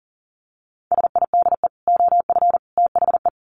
Morse code, heard as a keyed tone, about 40 wpm, 700 Hz. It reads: SIDE OF THE
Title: Sets of 3 Words 40wpm